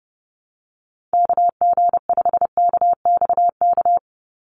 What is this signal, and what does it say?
Morse code, heard as a keyed tone, about 30 wpm, 700 Hz. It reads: KG5KXK
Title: USA Call Signs 30wpm